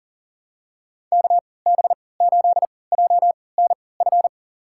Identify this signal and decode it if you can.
Morse code, heard as a keyed tone, about 40 wpm, spaced as for 23 wpm, 700 Hz. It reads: KB8JNF
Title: USA Call Signs - 6 Characters - 3x Character Spacing - 40wpm